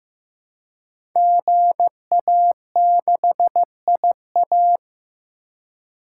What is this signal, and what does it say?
Morse code, heard as a keyed tone, about 15 wpm, 700 Hz. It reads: GA6IA
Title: Code Groups - 5 Letter:Numbers - 15wpm